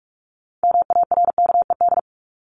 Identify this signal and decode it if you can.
Morse code, heard as a keyed tone, about 44 wpm, 700 Hz. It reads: MARKED
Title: ED Suffix - Top 100 Words with Base Words 45wpm